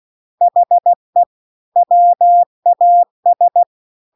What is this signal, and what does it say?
Morse code, heard as a keyed tone, about 16 wpm, 700 Hz. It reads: HE WAS